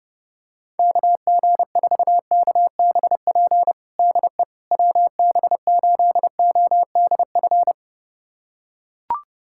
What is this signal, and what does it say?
Morse code, heard as a keyed tone, about 30 wpm, 700 Hz. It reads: KG4KBP DE WB8ODF
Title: USA Callsign DE Callsign 30wpm